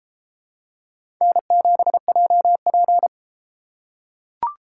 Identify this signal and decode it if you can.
Morse code, heard as a keyed tone, about 33 wpm, 700 Hz. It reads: N7JP